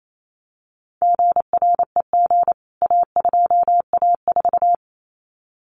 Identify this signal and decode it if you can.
Morse code, heard as a keyed tone, about 28 wpm, 700 Hz. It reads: GREG A2A4
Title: CWT Exchange 28wpm